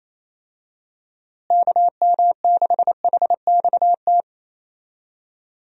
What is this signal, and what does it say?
Morse code, heard as a keyed tone, about 28 wpm, 700 Hz. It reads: KM6HXT
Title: USA Call Signs - 6 Characters - Repeated 2x - 28wpm